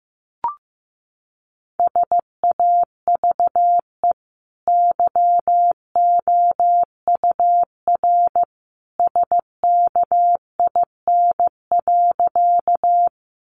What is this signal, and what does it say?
Morse code, heard as a keyed tone, about 15 wpm, 700 Hz. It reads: SAVE YOUR SKIN.
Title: Sentences from Top 1300 Words 15wpm